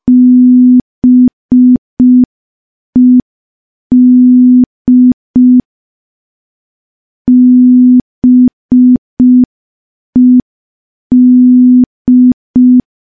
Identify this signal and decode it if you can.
Morse code, heard as a keyed tone, about 5 wpm, 250 Hz. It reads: BED BED
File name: morse / morse-training-05WPM